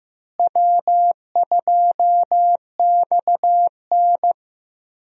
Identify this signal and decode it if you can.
Morse code, heard as a keyed tone, about 15 wpm, 700 Hz. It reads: W2XN
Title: USA Call Signs - 4 Characters - Repeated 2x - 15wpm